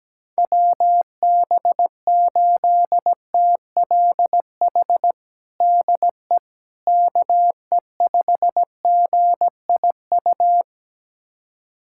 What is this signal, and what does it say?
Morse code, heard as a keyed tone, about 17 wpm, 700 Hz. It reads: WB8TLH DE KE5GIU
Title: USA Callsign DE Callsign - 6 Characters - 17wpm